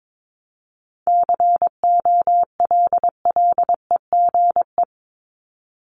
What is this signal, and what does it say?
Morse code, heard as a keyed tone, about 22 wpm, 700 Hz. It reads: COLLEGE
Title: Top 1001-1250 Words 22wpm